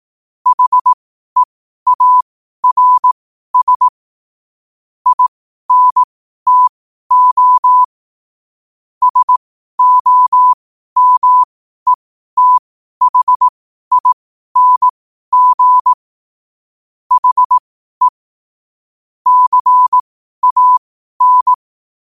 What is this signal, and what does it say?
Morse code, heard as a keyed tone, about 18 wpm, spaced as for 12 wpm, 1000 Hz. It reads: HEARS INTO SOMETHING HE CAN
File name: qotdfiles / 12wpm